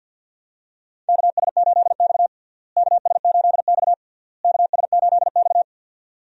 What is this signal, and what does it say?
Morse code, heard as a keyed tone, about 51 wpm, 700 Hz. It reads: KS8X KS8X KS8X
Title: USA Call Signs - 4 Characters - Repeated 3x - 50wpm